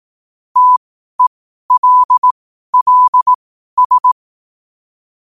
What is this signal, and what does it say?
Morse code, heard as a keyed tone, about 18 wpm, spaced as for 12 wpm, 1000 Hz. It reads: TELLS
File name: 12wpm